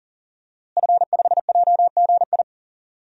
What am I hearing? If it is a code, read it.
Morse code, heard as a keyed tone, about 41 wpm, 700 Hz. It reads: F5JGI
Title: DXCC-100 Call Signs - 5 Characters - Repeated 2x - 40wpm